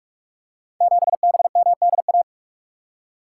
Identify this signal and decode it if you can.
Morse code, heard as a keyed tone, about 44 wpm, 690 Hz. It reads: 7BMDA
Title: Code Groups - 5 Letter:Numbers - 45wpm